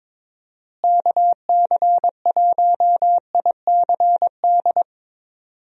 Morse code, heard as a keyed tone, about 22 wpm, 700 Hz. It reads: KC1ICD
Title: USA Call Signs - 6 Characters - Repeated 2x - 22wpm